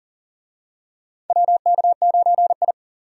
Morse code, heard as a keyed tone, about 40 wpm, 700 Hz. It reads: WK9I
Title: USA Call Signs - 4 Characters - 40wpm